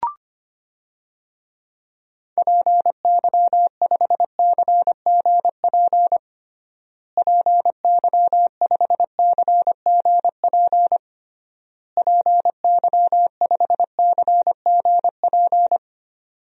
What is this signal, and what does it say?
Morse code, heard as a keyed tone, about 25 wpm, 700 Hz. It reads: PY5CGP PY5CGP PY5CGP
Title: DXCC-100 Call Signs - 6 Characters - Repeated 3x - 25wpm